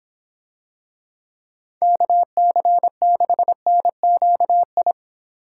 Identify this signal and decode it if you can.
Morse code, heard as a keyed tone, about 26 wpm, 700 Hz. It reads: KC6NQI